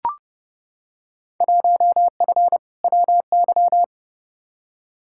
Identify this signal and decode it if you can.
Morse code, heard as a keyed tone, about 31 wpm, 700 Hz. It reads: 1F WY
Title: Field Day Exchanges Encore - 30wpm